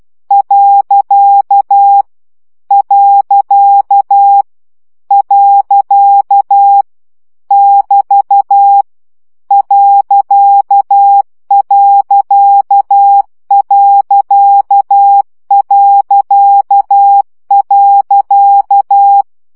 Morse code, heard as a keyed tone, about 12 wpm, 800 Hz. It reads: . . <BT> .....
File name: CW gyakorlas pont 10 baud